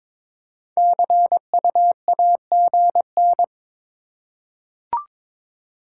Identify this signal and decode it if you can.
Morse code, heard as a keyed tone, about 22 wpm, 690 Hz. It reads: CU AGN
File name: practice / 22wpm / Top 20 QSO Elements - 2x Character Spacing - 22wpm